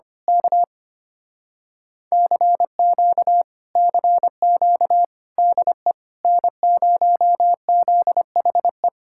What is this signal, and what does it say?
Morse code, heard as a keyed tone, about 24 wpm, 700 Hz. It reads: K CQ CQ DE N0ZHE
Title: Speed-Racing Common QSO Phrases - 25wpm